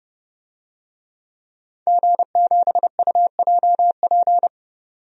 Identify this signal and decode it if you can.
Morse code, heard as a keyed tone, about 30 wpm, 700 Hz. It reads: G7UJP
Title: DXCC-100 Call Signs - 5 Characters - 30wpm